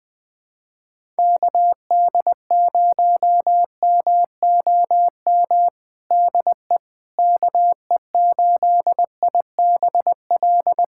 Morse code, heard as a keyed tone, about 20 wpm, 700 Hz. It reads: KD0MOM DE KE8IBL
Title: USA Callsign DE Callsign - 6 Characters - 20wpm